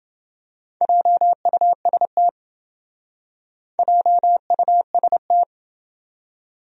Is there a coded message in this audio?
Yes: Morse code from 0.8 to 5.4 s, heard as a keyed tone, about 28 wpm, 700 Hz. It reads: JUST JUST